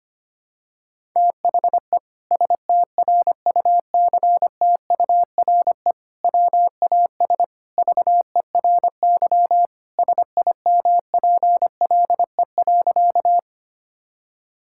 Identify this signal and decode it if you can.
Morse code, heard as a keyed tone, about 25 wpm, 700 Hz. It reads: THE STRUCTURE WAS VERY SIMPLE.
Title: Sentences from Top 1000 Words - Review Part 6 - 25wpm